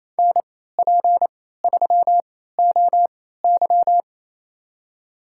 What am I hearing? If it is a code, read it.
Morse code, heard as a keyed tone, about 28 wpm, spaced as for 16 wpm, 700 Hz. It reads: NP3OY
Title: USA Call Signs - 3x Character Spacing - 28wpm